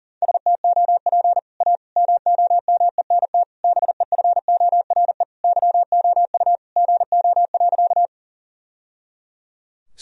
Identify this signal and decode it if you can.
Morse code, heard as a keyed tone, about 40 wpm, 700 Hz. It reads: STOP A MOMENT BEFORE YOU GO.
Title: Sentences from Top 1000 Words - Review Part 1 - 40wpm